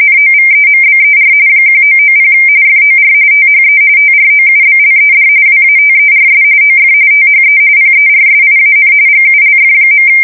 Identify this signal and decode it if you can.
RTTY, about 45.45 baud. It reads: NOVEMBER 1990 FOR THE TROUBLED WATERS OF THE PERSIAN GULF.
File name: fsk170 / USS-Missouri